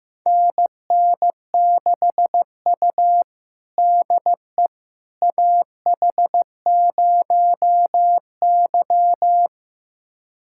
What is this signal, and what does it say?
Morse code, heard as a keyed tone, about 15 wpm, 700 Hz. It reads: NN6U DE AH0Y
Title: USA Callsign DE Callsign - 4 Characters - 15wpm